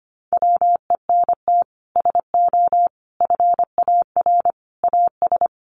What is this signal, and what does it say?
Morse code, heard as a keyed tone, about 25 wpm, 700 Hz. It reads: WENT SO FAR AS
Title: Sets of 5 Words Encore 25wpm